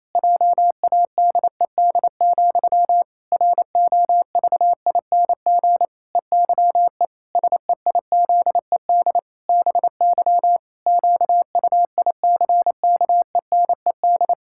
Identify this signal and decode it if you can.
Morse code, heard as a keyed tone, about 28 wpm, 700 Hz. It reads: JADED, ROVING EYE SEIZED BY QUICKENED
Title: Pangram Challenge v2 - 28wpm